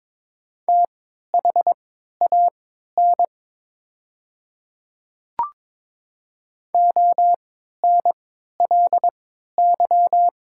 Morse code, heard as a keyed tone, about 23 wpm, spaced as for 13 wpm, 700 Hz. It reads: THAN ONLY
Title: 3x Character Spacing - Top 200 Words - 22wpm